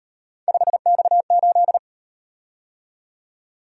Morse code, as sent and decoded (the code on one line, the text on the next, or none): ..... -..- ---..
5X8